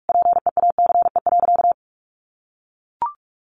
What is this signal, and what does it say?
Morse code, heard as a keyed tone, about 44 wpm, 700 Hz. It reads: PEACE.
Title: Sentences from Top 1500 Words 45wpm